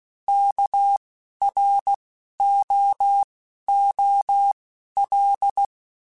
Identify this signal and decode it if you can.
Morse code, heard as a keyed tone, about 16 wpm, 800 Hz. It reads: K R O O L
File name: Alphabetical / Alphabetical mp3s / Ks @10